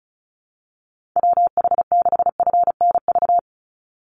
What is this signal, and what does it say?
Morse code, heard as a keyed tone, about 34 wpm, 700 Hz. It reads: WH6FNV
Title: USA Call Signs 35wpm